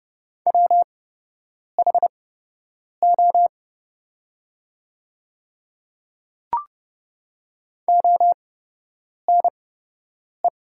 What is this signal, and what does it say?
Morse code, heard as a keyed tone, about 30 wpm, spaced as for 8 wpm, 700 Hz. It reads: WHO ONE
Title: 8x Character Spacing - Top 100 Words - 30wpm